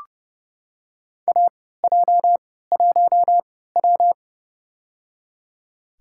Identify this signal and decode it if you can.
Morse code, heard as a keyed tone, about 30 wpm, spaced as for 17 wpm, 700 Hz. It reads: AJ1W
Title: USA Call Signs - 4 Characters - 3x Character Spacing - 30wpm